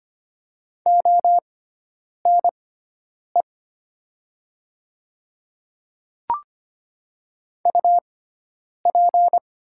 Morse code, heard as a keyed tone, about 25 wpm, spaced as for 9 wpm, 700 Hz. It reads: ONE UP
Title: 6x Character Spacing - Top 100 Words - 25wpm